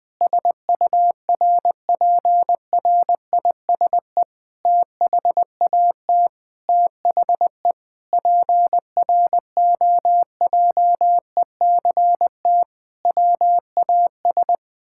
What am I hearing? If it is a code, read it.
Morse code, heard as a keyed tone, about 19 wpm, 700 Hz. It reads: SURPRISE THAT THE PROJECT WAS
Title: Sentences with 9-Word Phrases - 20wpm